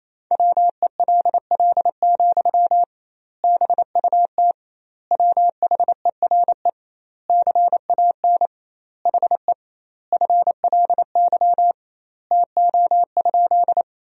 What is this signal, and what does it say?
Morse code, heard as a keyed tone, about 28 wpm, 700 Hz. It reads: WELL, BUT WHERE CAN HE FLY TO?
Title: 2x Word-Spacing - Sentences from Top 500 Words - 28wpm